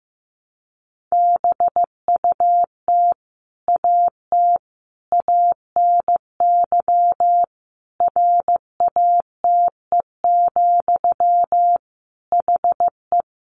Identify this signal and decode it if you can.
Morse code, heard as a keyed tone, about 14 wpm, 700 Hz. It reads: BUT AT ANY RATE, HE
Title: Sentences from Top 700 Words 15wpm